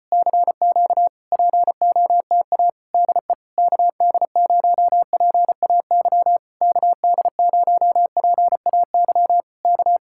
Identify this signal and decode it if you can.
Morse code, heard as a keyed tone, about 34 wpm, 700 Hz. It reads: CQ POTA DE KD0PAY KD0PAY K